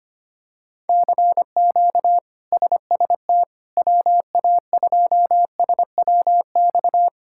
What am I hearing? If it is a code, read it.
Morse code, heard as a keyed tone, about 25 wpm, 700 Hz. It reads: CQ SST WA2SWX
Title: K1USN SST QSO Practice - 25wpm